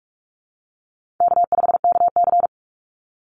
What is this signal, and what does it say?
Morse code, heard as a keyed tone, about 44 wpm, 700 Hz. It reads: K5KC
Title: Call Signs - 45wpm